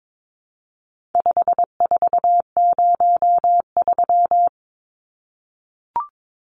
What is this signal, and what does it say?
Morse code, heard as a keyed tone, about 22 wpm, 700 Hz. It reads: 5403
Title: Numbers - 4-Digit 22wpm